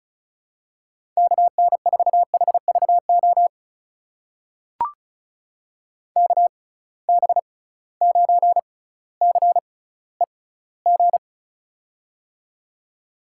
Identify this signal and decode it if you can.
Morse code, heard as a keyed tone, about 36 wpm, spaced as for 18 wpm, 690 Hz. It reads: KN4HVO KB9CEG
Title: USA Call Signs - 6 Characters - 6x Character Spacing - 35wpm